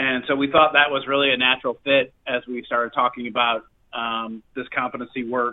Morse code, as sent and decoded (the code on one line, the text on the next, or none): none